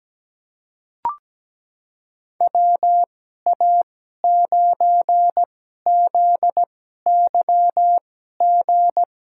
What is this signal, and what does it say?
Morse code, heard as a keyed tone, about 17 wpm, spaced as for 12 wpm, 700 Hz. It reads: WA9ZYG
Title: USA Call Signs - 2x Character Spacing - 17wpm